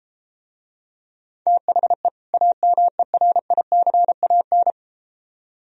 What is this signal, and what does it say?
Morse code, heard as a keyed tone, about 34 wpm, 700 Hz. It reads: THE AMERICAN